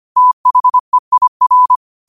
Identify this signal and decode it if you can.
Morse code, heard as a keyed tone, about 25 wpm, 1000 Hz. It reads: THEIR
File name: qotdfiles / 25wpm